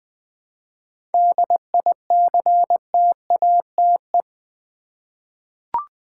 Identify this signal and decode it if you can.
Morse code, heard as a keyed tone, about 20 wpm, 700 Hz. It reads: DICTATE
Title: 7 Letter Words - First 3 Letters Given - 20wpm